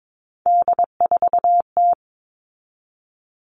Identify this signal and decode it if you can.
Morse code, heard as a keyed tone, about 22 wpm, 700 Hz. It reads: D4T
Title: All Cut Numbers - 3-Digit 22wpm